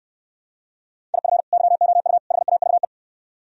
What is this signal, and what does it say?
Morse code, heard as a keyed tone, about 69 wpm, 700 Hz. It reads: IF YOU HAVE